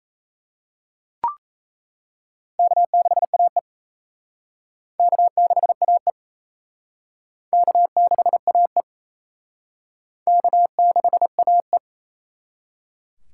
Morse code, heard as a keyed tone, about 36 wpm, 700 Hz. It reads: K6AE K6AE K6AE K6AE